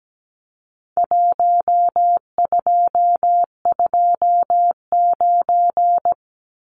Morse code, heard as a keyed tone, about 17 wpm, 700 Hz. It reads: 1229
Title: Numbers - Contest Style - 4-Digit 17wpm